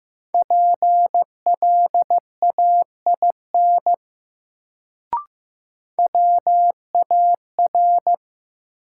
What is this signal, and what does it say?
Morse code, heard as a keyed tone, about 15 wpm, 700 Hz. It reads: PLAIN WAR
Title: Sentences from Top 400 Words - New Word Drill - 15wpm